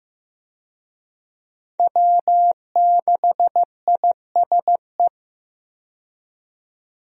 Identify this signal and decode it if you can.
Morse code, heard as a keyed tone, about 15 wpm, 700 Hz. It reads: W6ISE
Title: USA Call Signs - 5 Characters - 15wpm